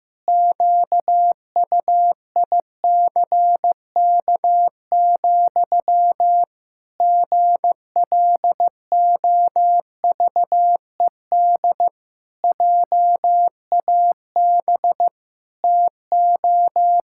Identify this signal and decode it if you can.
Morse code, heard as a keyed tone, about 15 wpm, 700 Hz. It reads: QUICK, GLOVED JAB TO
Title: Pangram Challenge v2 - 15wpm